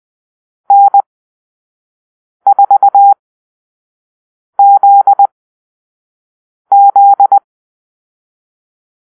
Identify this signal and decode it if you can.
Morse code, heard as a keyed tone, about 20 wpm, spaced as for 5 wpm, 800 Hz. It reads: N4ZZ
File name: Call-Sign Drill